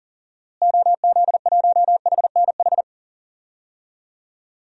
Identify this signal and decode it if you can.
Morse code, heard as a keyed tone, about 40 wpm, 700 Hz. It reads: OZ1HNH
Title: DXCC-100 Call Signs - 6 Characters - Repeated 3x - 40wpm